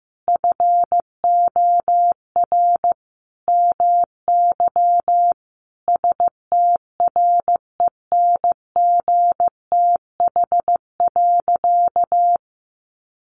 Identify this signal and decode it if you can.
Morse code, heard as a keyed tone, about 15 wpm, 700 Hz. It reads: FOR MY STRENGTH.
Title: Sentences from Top 1300 Words 15wpm